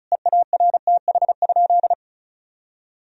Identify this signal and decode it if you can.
Morse code, heard as a keyed tone, about 34 wpm, 700 Hz. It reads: EARTH?
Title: Sentences from Top 1000 Words - Review Part 2 - 35wpm